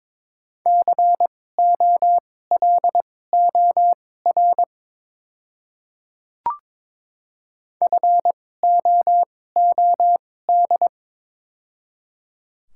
Morse code, heard as a keyed tone, about 22 wpm, spaced as for 16 wpm, 700 Hz. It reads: COLOR FOOD
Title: 2x Character Spacing - Top 500 Words - 22wpm